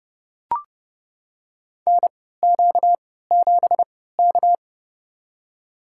Morse code, heard as a keyed tone, about 29 wpm, spaced as for 17 wpm, 700 Hz. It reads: NQ7K
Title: USA Call Signs - 4 Characters - 3x Character Spacing - 30wpm